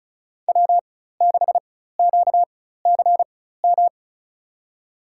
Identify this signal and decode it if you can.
Morse code, heard as a keyed tone, about 34 wpm, spaced as for 16 wpm, 700 Hz. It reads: W6QCM
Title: USA Call Signs - 5 Characters - 4x Character Spacing - 35wpm